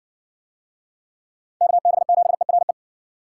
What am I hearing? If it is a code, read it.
Morse code, heard as a keyed tone, about 54 wpm, 700 Hz. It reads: KB7ERE